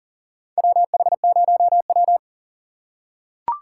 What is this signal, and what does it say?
Morse code, heard as a keyed tone, about 40 wpm, 700 Hz. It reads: WH0W
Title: USA Call Signs - 4 Characters - 3x Character Spacing - 40wpm